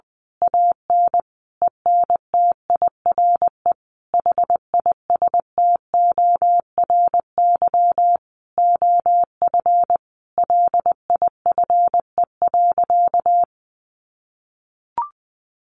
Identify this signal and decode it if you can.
Morse code, heard as a keyed tone, about 20 wpm, 700 Hz. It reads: AN ENTIRE HISTORY OF LIFE.
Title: A Sound Of Thunder 20wpm